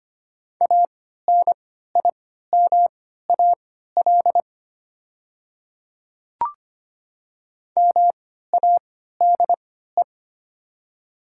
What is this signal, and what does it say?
Morse code, heard as a keyed tone, about 25 wpm, spaced as for 14 wpm, 700 Hz. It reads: ANIMAL MADE
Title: 3x Character Spacing - Top 300 Words - 25wpm